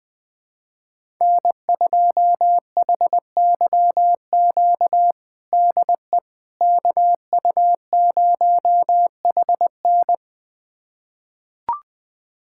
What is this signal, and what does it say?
Morse code, heard as a keyed tone, about 20 wpm, 700 Hz. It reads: N2HYQ DE KU0HN
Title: USA Callsign DE Callsign - 5 Characters - 20wpm